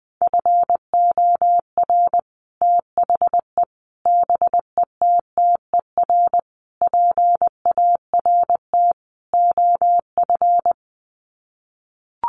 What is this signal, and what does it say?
Morse code, heard as a keyed tone, about 20 wpm, 700 Hz. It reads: FOR THE BETTER PART OF
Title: Sets of 5 Words Encore2 20wpm